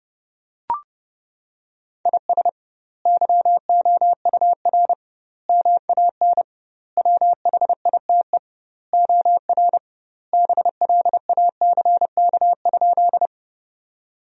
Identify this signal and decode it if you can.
Morse code, heard as a keyed tone, about 30 wpm, 700 Hz. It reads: IS YOUR MAN WHITE OR BLACK?
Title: 2x Word-Spacing - Sentences from Top 400 Words - 30wpm